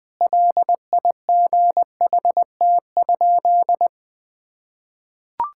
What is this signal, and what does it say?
Morse code, heard as a keyed tone, about 20 wpm, 700 Hz. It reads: LIGHT?
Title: Sentences from Top 1000 Words - Review Part 5 - 20wpm